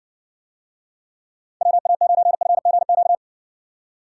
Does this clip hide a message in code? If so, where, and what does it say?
Yes: Morse code from 1.6 to 3.2 s, heard as a keyed tone, about 58 wpm, 700 Hz. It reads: WA9FGQ